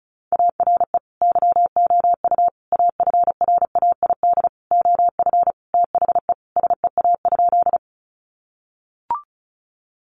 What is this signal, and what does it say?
Morse code, heard as a keyed tone, about 34 wpm, 700 Hz. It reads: ARE YOU AFRAID OF THE SEA?
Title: Sentences from Top 1500 Words 35wpm